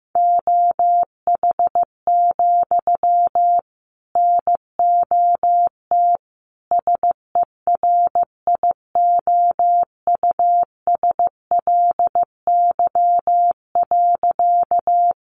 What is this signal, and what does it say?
Morse code, heard as a keyed tone, about 15 wpm, 700 Hz. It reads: OH, NOT SERIOUSLY.
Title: Sentences from Top 1200 Words 15wpm